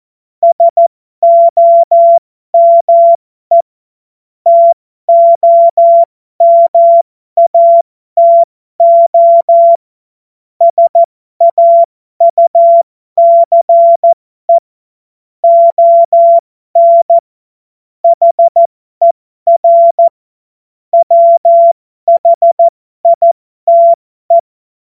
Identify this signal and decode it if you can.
Morse code, heard as a keyed tone, about 14 wpm, spaced as for 12 wpm, 680 Hz. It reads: SOME TOMATO SAUCE ON HER WHITE
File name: Morsekurs / Witze